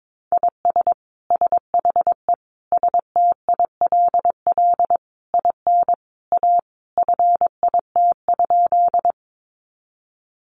Morse code, heard as a keyed tone, about 22 wpm, 700 Hz. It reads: IS SHE STILL IN A FIT?
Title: Sentences from Top 800 Words 22wpm